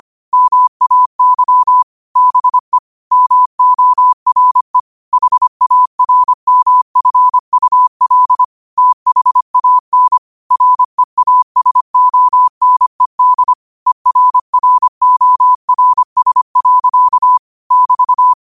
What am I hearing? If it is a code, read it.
Morse code, heard as a keyed tone, about 25 wpm, 1000 Hz. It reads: MAY BE MORE HARMFUL THAN REASONED ERRORS. <BT>